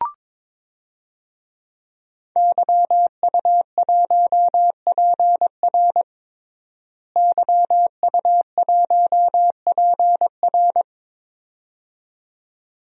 Morse code, heard as a keyed tone, about 22 wpm, 700 Hz. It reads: YU1PR YU1PR
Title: DXCC-100 Call Signs - 5 Characters - Repeated 2x - 22wpm